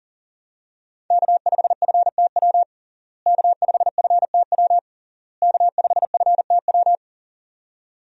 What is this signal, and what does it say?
Morse code, heard as a keyed tone, about 40 wpm, 700 Hz. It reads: K5FTW K5FTW K5FTW
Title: USA Call Signs - 5 Characters - Repeated 3x - 40wpm